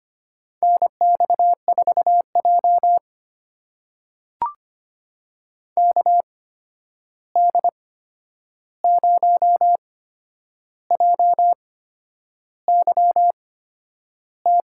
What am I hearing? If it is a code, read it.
Morse code, heard as a keyed tone, about 25 wpm, spaced as for 9 wpm, 700 Hz. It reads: NX4J KD0JYT
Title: USA Call Signs - 8x Character Spacing - 25wpm